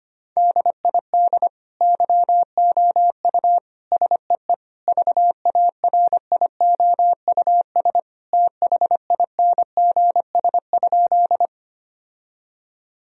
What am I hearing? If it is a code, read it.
Morse code, heard as a keyed tone, about 25 wpm, 700 Hz. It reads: DID YOU SEE VARIOUS THINGS?
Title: Sentences from Top 1000 Words - Review Part 4 - 25wpm